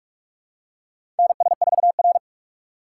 Morse code, heard as a keyed tone, about 46 wpm, 700 Hz. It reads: NS4R
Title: USA Call Signs - 4x Character Spacing - 45wpm